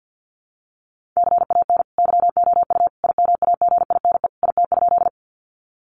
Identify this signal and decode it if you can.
Morse code, heard as a keyed tone, about 50 wpm, 700 Hz. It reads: CAN YOU IMAGINE IT?
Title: Sentences from Top 1400 Words 50wpm